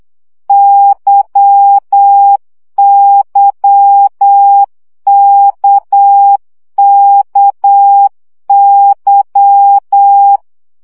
Morse code, heard as a keyed tone, about 8 wpm, 800 Hz. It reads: YYKKY
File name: CW gyakorlas k c y 7 baud